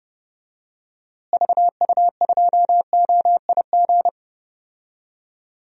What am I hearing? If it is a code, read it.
Morse code, heard as a keyed tone, about 32 wpm, 700 Hz. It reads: VU2OIG
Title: DXCC-100 Call Signs - 6 Characters - Repeated 2x - 30wpm